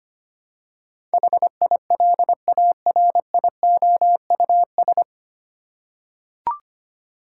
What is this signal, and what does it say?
Morse code, heard as a keyed tone, about 25 wpm, 700 Hz. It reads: HILARIOUS
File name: practice / 25wpm / 9 Letter Words - First 3 Letters Given - 25wpm